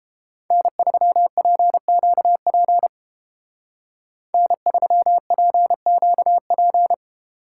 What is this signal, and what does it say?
Morse code, heard as a keyed tone, about 30 wpm, 700 Hz. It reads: N3PQP N3PQP